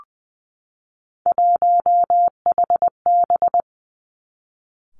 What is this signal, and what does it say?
Morse code, heard as a keyed tone, about 20 wpm, 700 Hz. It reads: 1HB
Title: Three Challenging Letter-Number Combinations 20wpm